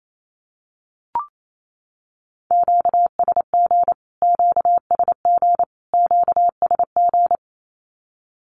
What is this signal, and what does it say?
Morse code, heard as a keyed tone, about 28 wpm, 700 Hz. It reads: QSG QSG QSG
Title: Q-Codes Repeated 3x 28wpm